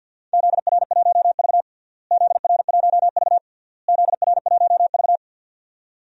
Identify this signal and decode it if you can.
Morse code, heard as a keyed tone, about 50 wpm, 700 Hz. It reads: ZR1V ZR1V ZR1V
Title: DXCC-100 Call Signs - 4 Characters - Repeated 3x - 50wpm